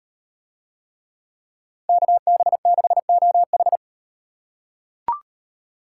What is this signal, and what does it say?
Morse code, heard as a keyed tone, about 38 wpm, 700 Hz. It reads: KB6OH